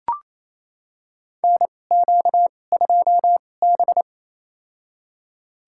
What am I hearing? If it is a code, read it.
Morse code, heard as a keyed tone, about 28 wpm, spaced as for 20 wpm, 700 Hz. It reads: NQ2B
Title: USA Call Signs - 4 Characters - 2x Character Spacing - 28wpm